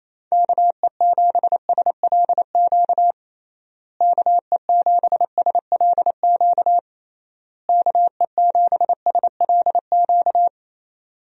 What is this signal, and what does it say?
Morse code, heard as a keyed tone, about 28 wpm, 700 Hz. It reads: KE7SLQ KE7SLQ KE7SLQ
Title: USA Call Signs - 6 Characters - Repeated 3x - 28wpm